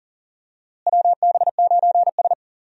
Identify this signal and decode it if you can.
Morse code, heard as a keyed tone, about 40 wpm, 700 Hz. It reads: WB9S
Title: USA Call Signs - 4 Characters - Repeated 2x - 40wpm